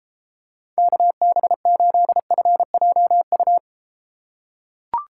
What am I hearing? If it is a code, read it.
Morse code, heard as a keyed tone, about 30 wpm, 700 Hz. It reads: KB8FJU